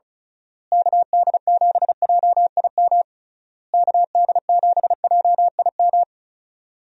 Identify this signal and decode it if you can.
Morse code, heard as a keyed tone, about 36 wpm, 700 Hz. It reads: KD7JIM KD7JIM
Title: USA Call Signs - 6 Characters - Repeated 3x - 35wpm